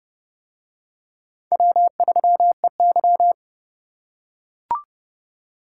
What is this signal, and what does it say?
Morse code, heard as a keyed tone, about 29 wpm, 700 Hz. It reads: W3EY